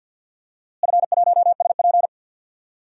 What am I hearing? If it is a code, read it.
Morse code, heard as a keyed tone, about 48 wpm, 700 Hz. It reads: F1SP
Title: DXCC-100 Call Signs - 4 Characters - Repeated 2x - 50wpm